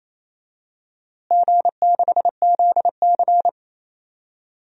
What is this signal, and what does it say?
Morse code, heard as a keyed tone, about 28 wpm, 700 Hz. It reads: G6ZC
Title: DXCC-100 Call Signs - 4 Characters - Repeated 2x - 28wpm